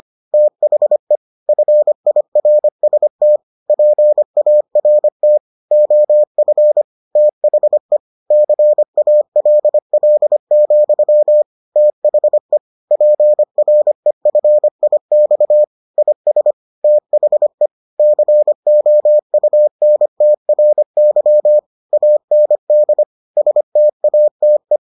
Morse code, heard as a keyed tone, about 25 wpm, 600 Hz. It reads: THE FIRST PART OF THE CALL, THE PREFIX IS THE COUNTRY AND STATE